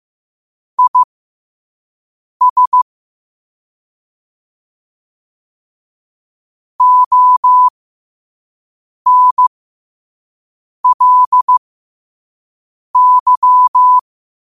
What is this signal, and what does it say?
Morse code, heard as a keyed tone, about 15 wpm, spaced as for 5 wpm, 1000 Hz. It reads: IS ONLY
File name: qotdfiles / 5wpm